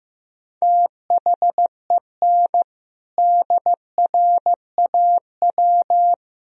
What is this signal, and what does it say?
Morse code, heard as a keyed tone, about 15 wpm, 700 Hz. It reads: THEN DRAW
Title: Sentences from Top 1000 Words - Review Part 1 - 15wpm